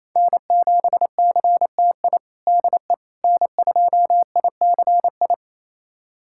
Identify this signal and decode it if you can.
Morse code, heard as a keyed tone, about 28 wpm, 700 Hz. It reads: N7CTI DE N2ICI
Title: USA Callsign DE Callsign - 5 Characters - 28wpm